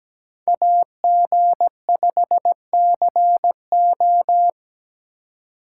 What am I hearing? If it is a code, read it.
Morse code, heard as a keyed tone, about 20 wpm, 700 Hz. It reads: AG5CO
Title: Speed Racing - USA 2x2 Call Signs 17wpm